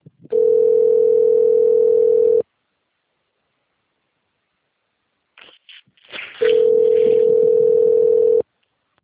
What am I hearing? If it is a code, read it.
Ringback tone (North America), 440 + 480 Hz.
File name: rg-1600-unknown- - - .21098